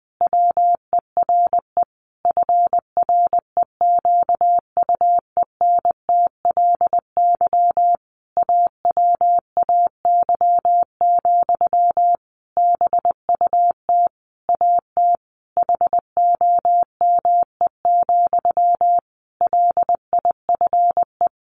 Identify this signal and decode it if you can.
Morse code, heard as a keyed tone, about 20 wpm, 700 Hz. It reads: WERE FREQUENTLY AWAY, BUT AT HOME, LIFE
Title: Sentences from Top 1400 Words 20wpm